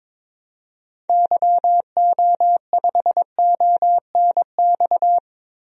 Morse code, heard as a keyed tone, about 22 wpm, 700 Hz. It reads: YO5ONX
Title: DXCC-100 Call Signs - 6 Characters - Repeated 2x - 22wpm